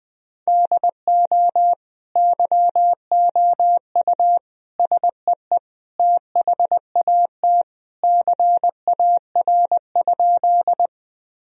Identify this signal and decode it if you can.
Morse code, heard as a keyed tone, about 20 wpm, 700 Hz. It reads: DO YOU SEE THAT CAR?
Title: Sentences from Top 1000 Words - Review Part 1 - 20wpm